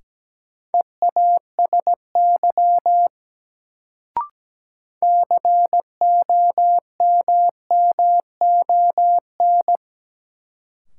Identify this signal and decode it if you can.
Morse code, heard as a keyed tone, about 17 wpm, 700 Hz. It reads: EASY COMMON
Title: Sentences from Top 700 Words - New Word Drill - 17wpm